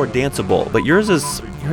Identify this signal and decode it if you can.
Speech. There is no code.